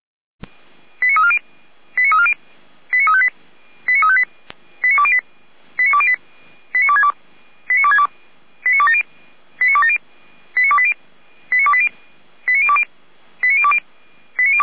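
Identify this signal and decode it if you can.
Five-tone selective call (ZVEI), heard as quick runs of five tones, tones of about 70 ms. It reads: 89340 89340 89358 89358 89208 89262 89262 89270 89270 89280 89280 89020 89020
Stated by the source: ZVEI five-tone call received by SDR, fire brigade Orlamünde (feuerwehr-orlamuende).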